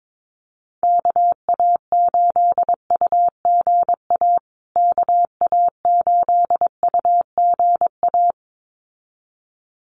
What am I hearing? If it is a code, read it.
Morse code, heard as a keyed tone, about 22 wpm, 700 Hz. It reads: KA8UGA KA8UGA
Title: USA Call Signs Repeated 2x 22wpm